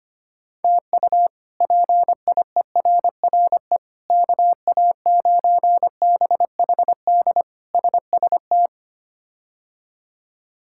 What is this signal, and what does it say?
Morse code, heard as a keyed tone, about 25 wpm, 700 Hz. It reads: TU PIERRE KA9BHD SST
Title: K1USN SST QSO Practice - 25wpm